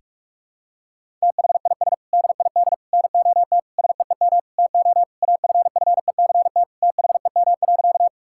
Morse code, heard as a keyed tone, about 41 wpm, 700 Hz. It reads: THIS DID NOT SEEM TO AFFECT THEM.
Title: Sentences from Top 1000 Words - Review Part 6 - 45wpm